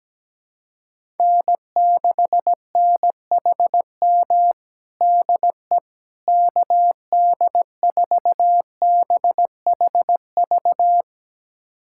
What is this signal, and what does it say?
Morse code, heard as a keyed tone, about 17 wpm, 700 Hz. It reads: N6NHM DE KD4BHV
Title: USA Callsign DE Callsign 17wpm